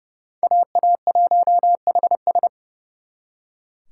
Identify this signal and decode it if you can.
Morse code, heard as a keyed tone, about 40 wpm, 700 Hz. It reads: AA1HS